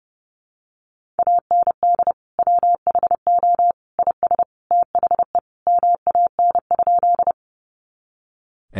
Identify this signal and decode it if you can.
Morse code, heard as a keyed tone, about 28 wpm, 700 Hz. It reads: AND WHO IS THE MAN?
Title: Sentences from Top 200 Words 30wpm